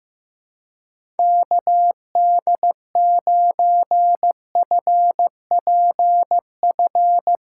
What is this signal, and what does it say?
Morse code, heard as a keyed tone, about 15 wpm, 700 Hz. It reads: KD9FPF
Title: USA Callsign DE Callsign - 6 Characters - 15wpm